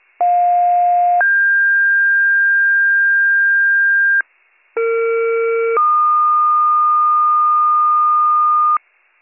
Two-tone page, heard as a one-second tone, then a three-second tone, about 710 then 1600 Hz, about 460 then 1220 Hz, with noise. It is heard from 0.2 to 8.8 s.